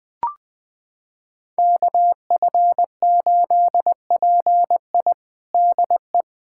Morse code, heard as a keyed tone, about 21 wpm, 700 Hz. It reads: KF8PI DE
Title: USA Callsign DE Callsign - 5 Characters - 20wpm